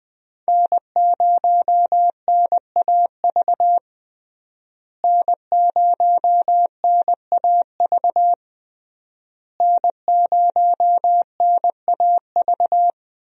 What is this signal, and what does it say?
Morse code, heard as a keyed tone, about 20 wpm, 700 Hz. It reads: N0NAV N0NAV N0NAV
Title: USA Call Signs - 5 Characters - Repeated 3x - 20wpm